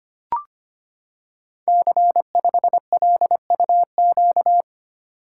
Morse code, heard as a keyed tone, about 25 wpm, 700 Hz. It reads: C5LUQ
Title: Code Groups - 5 Letter:Numbers - 25wpm